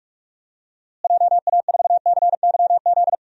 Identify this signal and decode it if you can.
Morse code, heard as a keyed tone, about 46 wpm, 700 Hz. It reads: JA4CYZ
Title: DXCC-100 Call Signs - 6 Characters - Repeated 3x - 45wpm